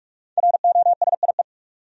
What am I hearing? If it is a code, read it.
Morse code, heard as a keyed tone, about 42 wpm, 700 Hz. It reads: ROSIE
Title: USA Common Names - 45wpm